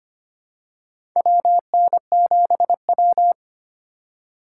Morse code, heard as a keyed tone, about 25 wpm, 700 Hz. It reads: WN7W
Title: USA Call Signs - 4x Character Spacing - 25wpm